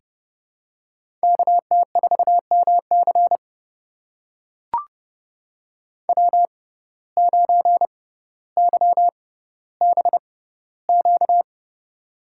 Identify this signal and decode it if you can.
Morse code, heard as a keyed tone, about 30 wpm, spaced as for 15 wpm, 700 Hz. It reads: KT4MC W9YBQ
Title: USA Call Signs - 5 Characters - 6x Character Spacing - 30wpm